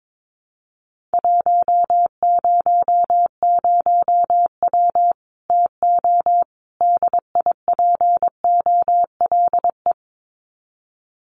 Morse code, heard as a keyed tone, about 22 wpm, 700 Hz. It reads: 100W TO DIPOLE
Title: Common QSO Phrases 22wpm